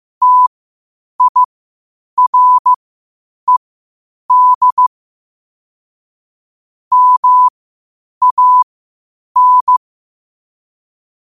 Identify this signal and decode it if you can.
Morse code, heard as a keyed tone, about 15 wpm, spaced as for 8 wpm, 1000 Hz. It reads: TIRED MAN
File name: qotdfiles / 8wpm